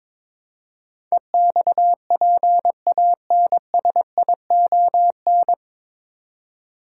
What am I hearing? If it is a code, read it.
Morse code, heard as a keyed tone, about 22 wpm, 700 Hz. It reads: EXPANSION
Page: Morse Code Ninja practice 22wpm